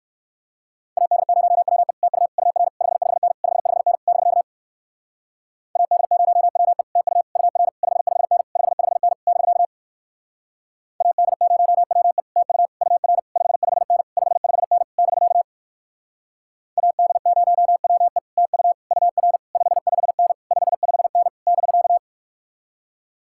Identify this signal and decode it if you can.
Morse code, heard as a keyed tone, about 63 wpm, 700 Hz. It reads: AD0WE TU UR 55N 55N <BK> AD0WE TU UR 55N 55N <BK> AD0WE TU UR 55N 55N <BK> AD0WE TU UR 55N 55N <BK>